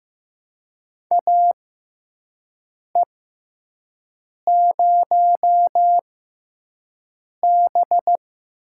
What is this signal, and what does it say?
Morse code, heard as a keyed tone, about 15 wpm, spaced as for 5 wpm, 700 Hz. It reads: AE0B
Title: USA Call Signs - 6x Character Spacing - 15wpm